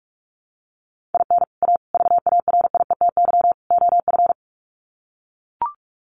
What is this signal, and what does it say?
Morse code, heard as a keyed tone, about 44 wpm, 700 Hz. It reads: IN A VARIETY OF
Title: Sets of 4 Words 45wpm